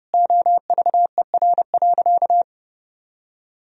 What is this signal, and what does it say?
Morse code, heard as a keyed tone, about 30 wpm, 700 Hz. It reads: OVER.
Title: Sentences from Top 1000 Words - Review Part 3 - 30wpm